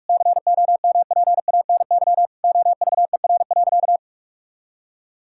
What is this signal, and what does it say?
Morse code, heard as a keyed tone, about 46 wpm, 700 Hz. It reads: COMPANY OVER.
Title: Sentences from Top 1000 Words - Review Part 6 - 45wpm